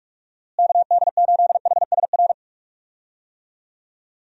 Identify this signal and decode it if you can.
Morse code, heard as a keyed tone, about 41 wpm, 700 Hz. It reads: KD8HSR
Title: USA Call Signs - 6 Characters - Repeated 2x - 45wpm